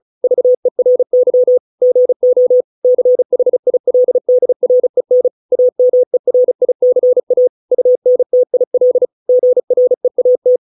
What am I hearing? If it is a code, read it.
Morse code, heard as a keyed tone, about 34 wpm, 500 Hz. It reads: VERY GO CHILDREN AMERICA UNTIL GREAT